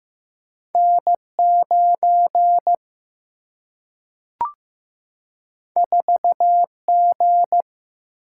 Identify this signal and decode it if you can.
Morse code, heard as a keyed tone, about 15 wpm, 700 Hz. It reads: N9 4G
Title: Single-Letter Single-Number 15wpm